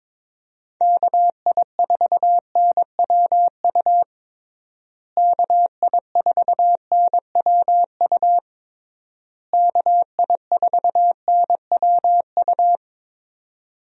Morse code, heard as a keyed tone, about 22 wpm, 700 Hz. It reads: KI4NWU KI4NWU KI4NWU
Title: USA Call Signs - 6 Characters - Repeated 3x - 22wpm